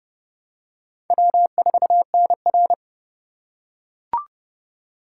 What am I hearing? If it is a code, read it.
Morse code, heard as a keyed tone, about 30 wpm, 700 Hz. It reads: W4NR
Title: USA Call Signs - 4 Characters - 6x Character Spacing - 30wpm